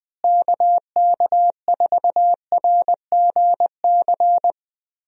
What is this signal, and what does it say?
Morse code, heard as a keyed tone, about 21 wpm, 700 Hz. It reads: KK4RGC
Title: USA Call Signs - 6 Characters - Repeated 3x - 20wpm